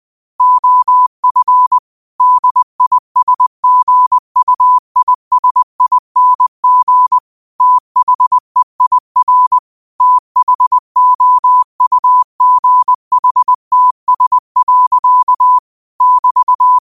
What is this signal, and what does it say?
Morse code, heard as a keyed tone, about 20 wpm, 1000 Hz. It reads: OF DISGUISING THEIR THOUGHTS. <BT>
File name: qotdfiles / 20wpm